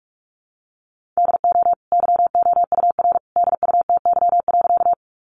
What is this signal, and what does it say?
Morse code, heard as a keyed tone, about 44 wpm, 700 Hz. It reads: DO YOUR DUTY.
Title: Sentences from Top 1500 Words 45wpm